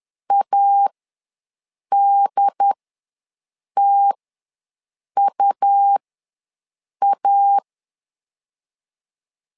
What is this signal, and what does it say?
Morse code, heard as a keyed tone, about 10 wpm, spaced as for 6 wpm, 800 Hz. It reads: ADTUA
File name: LEZIONI / 2 UHDN / 2 UHD ws 10wpm 2058